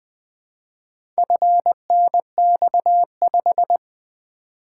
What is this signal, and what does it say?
Morse code, heard as a keyed tone, about 20 wpm, 700 Hz. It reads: FNX5
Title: Four Letter-Number Combinations - 20wpm